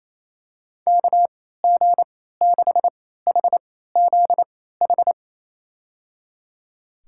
Morse code, heard as a keyed tone, about 28 wpm, spaced as for 16 wpm, 700 Hz. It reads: KG6HZH
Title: USA Call Signs - 3x Character Spacing - 28wpm